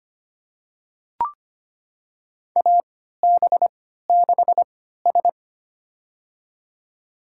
Morse code, heard as a keyed tone, about 25 wpm, spaced as for 14 wpm, 700 Hz. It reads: AB6S
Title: USA Call Signs - 4 Characters - 3x Character Spacing - 25wpm